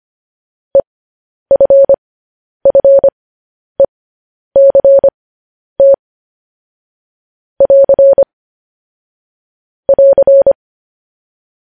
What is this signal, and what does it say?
Morse code, heard as a keyed tone, about 25 wpm, spaced as for 10 wpm, 550 Hz. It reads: EFFECT <AR> <AR>
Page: news headlines in morse code at 10 wpm